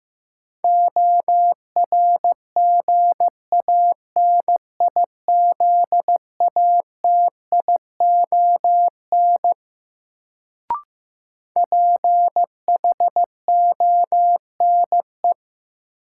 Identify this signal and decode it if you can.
Morse code, heard as a keyed tone, about 15 wpm, 700 Hz. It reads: ORGANIZATION PHONE
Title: Sentences from Top 1100 Words - New Word Drill Alt - 15wpm